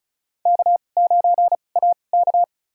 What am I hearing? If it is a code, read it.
Morse code, heard as a keyed tone, about 34 wpm, spaced as for 25 wpm, 700 Hz. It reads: K9AK
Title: USA Call Signs - 4 Characters - 2x Character Spacing - 35wpm